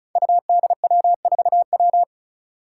Morse code, heard as a keyed tone, about 36 wpm, 700 Hz. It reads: UDW4W
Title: Five Challenging Letter-Number Combinations 35wpm